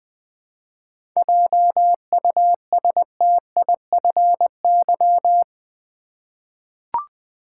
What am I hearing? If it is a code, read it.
Morse code, heard as a keyed tone, about 20 wpm, 700 Hz. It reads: JUSTIFY
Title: Seven Letter Words - 20wpm